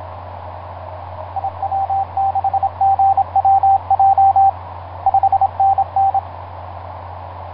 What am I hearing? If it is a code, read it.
Morse code, heard as a keyed tone, about 27 wpm, 780 Hz, with radio noise. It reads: IW6GWJ 5NN
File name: vk9x-g6ay ha7lw etc 10m cw 23feb2010 0836